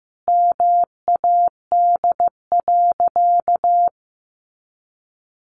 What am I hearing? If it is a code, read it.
Morse code, heard as a keyed tone, about 15 wpm, 700 Hz. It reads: MAD.
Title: Additional Easy Phrases 15wpm